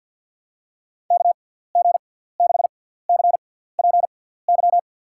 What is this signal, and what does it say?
Morse code, heard as a keyed tone, about 52 wpm, spaced as for 17 wpm, 700 Hz. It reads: KG6CPY
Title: USA Call Signs - 6 Characters - 6x Character Spacing - 50wpm